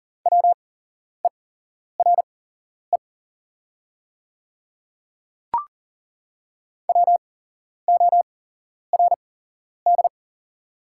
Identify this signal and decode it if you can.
Morse code, heard as a keyed tone, about 41 wpm, spaced as for 11 wpm, 700 Hz. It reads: WERE WORD
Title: 8x Character Spacing - Top 100 Words - 40wpm